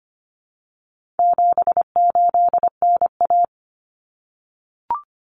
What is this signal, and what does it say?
Morse code, heard as a keyed tone, about 25 wpm, 700 Hz. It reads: 78NA
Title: Common Cut Numbers - 4-Digit 25wpm